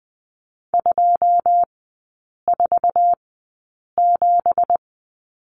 Morse code, heard as a keyed tone, about 20 wpm, spaced as for 8 wpm, 700 Hz. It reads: 247
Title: 178 Lesson 29e - Slash - Three Characters 20wpm